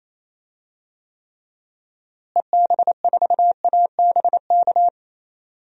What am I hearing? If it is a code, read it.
Morse code, heard as a keyed tone, about 28 wpm, 700 Hz. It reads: EB4ABK
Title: DXCC-100 Call Signs - 6 Characters - Repeated 3x - 28wpm